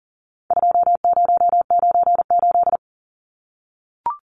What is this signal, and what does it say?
Morse code, heard as a keyed tone, about 40 wpm, 700 Hz. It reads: 2098
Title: Numbers - 4-Digit 40wpm